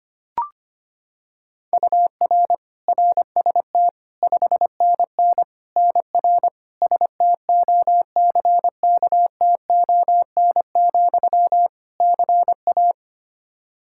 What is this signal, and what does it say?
Morse code, heard as a keyed tone, about 25 wpm, 700 Hz. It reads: UR RST 5NN NR STOCKTON, CA
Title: Practice QSO - Exchange 1 - Beginning - 25wpm